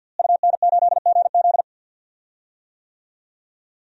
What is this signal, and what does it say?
Morse code, heard as a keyed tone, about 46 wpm, 700 Hz. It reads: UN8GZ
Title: DXCC-100 Call Signs - 5 Characters - Repeated 3x - 50wpm